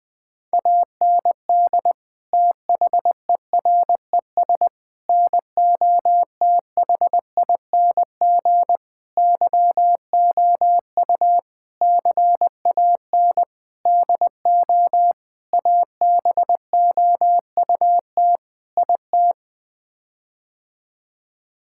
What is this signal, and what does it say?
Morse code, heard as a keyed tone, about 21 wpm, 700 Hz. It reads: AND THERES NOTHING YOU CAN DO ABOUT IT
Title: Sets of 8 Words - 20wpm